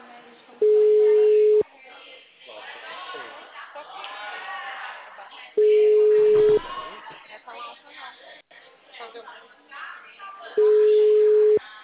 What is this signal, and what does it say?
Ringback tone (Europe, 425 Hz), with speech.